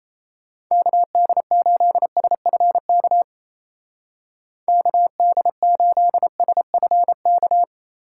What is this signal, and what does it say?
Morse code, heard as a keyed tone, about 36 wpm, 700 Hz. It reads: KD8SFK KD8SFK